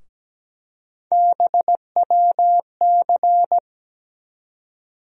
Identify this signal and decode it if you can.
Morse code, heard as a keyed tone, about 17 wpm, 700 Hz. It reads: BWC
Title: USA Call Sign Postfixes - 3 Letters 17wpm